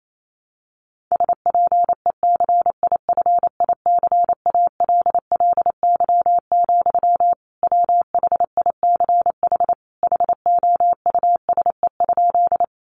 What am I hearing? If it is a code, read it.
Morse code, heard as a keyed tone, about 28 wpm, 700 Hz. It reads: SPECIFICALLY, WHICH HOUSE?
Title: Sentences from Top 1100 Words 28wpm